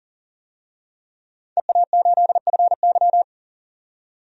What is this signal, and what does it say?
Morse code, heard as a keyed tone, about 39 wpm, 700 Hz. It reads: EA8FY
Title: DXCC-100 Call Signs - 5 Characters - 40wpm